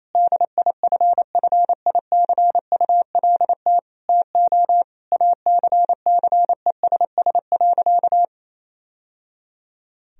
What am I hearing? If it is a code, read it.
Morse code, heard as a keyed tone, about 28 wpm, 700 Hz. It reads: DIFFICULT TO ACCESS.
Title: Sentences from Top 1000 Words - Review Part 4 - 28wpm